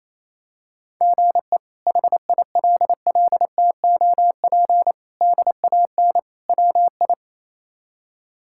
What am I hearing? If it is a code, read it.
Morse code, heard as a keyed tone, about 27 wpm, 700 Hz. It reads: GE HILLTOP DAN WI
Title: K1USN SST QSO Practice - 28wpm